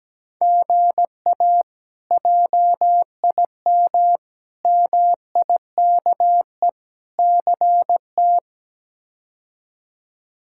Morse code, heard as a keyed tone, about 17 wpm, 700 Hz. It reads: GA JIM MIKE CT
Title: K1USN SST QSO Practice - 17wpm